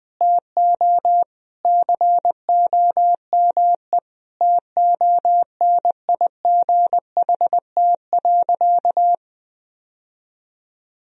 Morse code, heard as a keyed tone, about 20 wpm, 700 Hz. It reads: TO COME TONIGHT.
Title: Sentences from Top 1000 Words - Review Part 3 - 20wpm